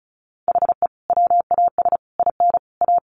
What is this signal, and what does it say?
Morse code, heard as a keyed tone, about 34 wpm, 700 Hz. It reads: HE WAS IN A
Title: Sentences from Top 1200 Words 35wpm